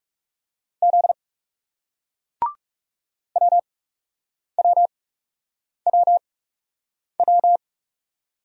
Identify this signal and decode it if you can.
Morse code, heard as a keyed tone, about 42 wpm, spaced as for 8 wpm, 700 Hz. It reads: Z WWWW